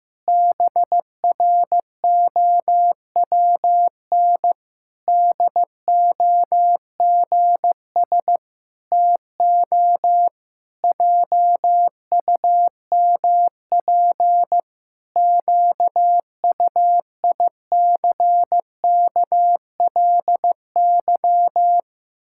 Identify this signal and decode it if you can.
Morse code, heard as a keyed tone, about 15 wpm, 700 Hz. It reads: BROWN DOGS TO JUMP QUICKLY
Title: Pangram Challenge v2 - 15wpm